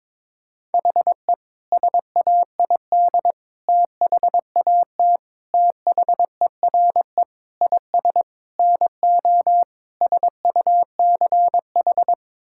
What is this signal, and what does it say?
Morse code, heard as a keyed tone, about 22 wpm, 700 Hz. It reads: HE SAID THAT THERE IS NO SUCH
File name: practice / 22wpm / Sentences with 8-Word Phrases - 22wpm